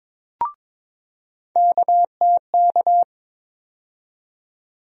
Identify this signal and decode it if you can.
Morse code, heard as a keyed tone, about 22 wpm, 700 Hz. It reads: KTK
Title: USA Call Sign Postfixes - 3 Letters 22wpm